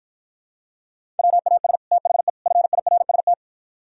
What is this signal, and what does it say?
Morse code, heard as a keyed tone, about 52 wpm, 700 Hz. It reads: WAS THE FIRST